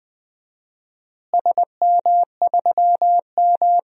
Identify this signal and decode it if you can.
Morse code, heard as a keyed tone, about 21 wpm, 700 Hz. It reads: SM3M
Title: DXCC-100 Call Signs - 4 Characters - Repeated 2x - 20wpm